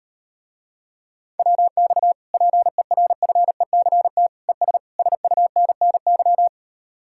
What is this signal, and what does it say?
Morse code, heard as a keyed tone, about 38 wpm, 700 Hz. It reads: WX PERFECT ES SUNNY